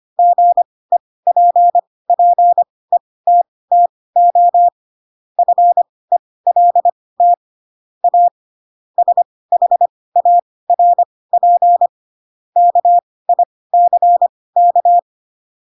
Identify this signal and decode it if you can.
Morse code, heard as a keyed tone, about 25 wpm, spaced as for 18 wpm, 700 Hz. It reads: GEPPETTO FELT A SHARP KICK
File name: Pinocchio ch3 12 25 ik1hge